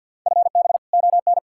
Morse code, heard as a keyed tone, about 64 wpm, 700 Hz. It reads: FB ON